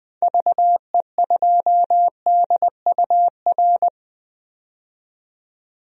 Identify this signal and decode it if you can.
Morse code, heard as a keyed tone, about 21 wpm, 700 Hz. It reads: VE2DUR
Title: DXCC-100 Call Signs - 6 Characters - 20wpm